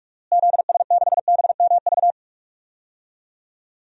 Morse code, heard as a keyed tone, about 46 wpm, 700 Hz. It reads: ZS6BMV
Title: DXCC-100 Call Signs - 6 Characters - Repeated 3x - 45wpm